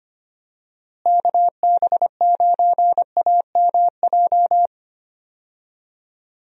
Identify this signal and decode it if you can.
Morse code, heard as a keyed tone, about 25 wpm, 700 Hz. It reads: KB9AMJ
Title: USA Call Signs - 6 Characters - Repeated 3x - 25wpm